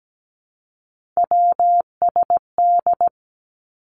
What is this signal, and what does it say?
Morse code, heard as a keyed tone, about 17 wpm, 700 Hz. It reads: WSD
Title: Three Challenging Letter-Number Combinations 17wpm